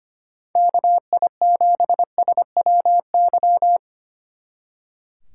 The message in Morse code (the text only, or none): KI7SWY